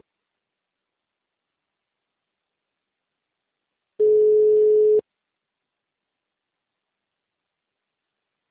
Ringback tone (Europe, 425 Hz).